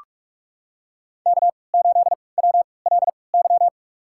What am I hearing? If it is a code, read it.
Morse code, heard as a keyed tone, about 46 wpm, spaced as for 25 wpm, 700 Hz. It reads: K8WLY
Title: USA Call Signs - 5 Characters - 3x Character Spacing - 45wpm